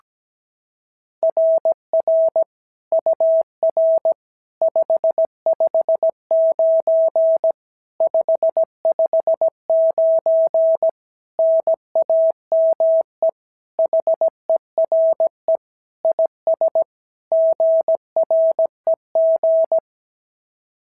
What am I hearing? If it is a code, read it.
Morse code, heard as a keyed tone, about 16 wpm, 650 Hz. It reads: RR UR 559 559 NAME HERE IS GREG
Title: SOTA QSO Practice 17wpm